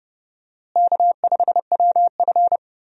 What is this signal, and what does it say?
Morse code, heard as a keyed tone, about 30 wpm, 700 Hz. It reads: K5WF
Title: USA Call Signs - 4 Characters - Repeated 3x - 30wpm